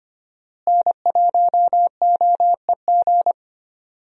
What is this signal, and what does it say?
Morse code, heard as a keyed tone, about 25 wpm, 700 Hz. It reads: N1OEG
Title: USA Call Signs - 5 Characters - Repeated 2x - 25wpm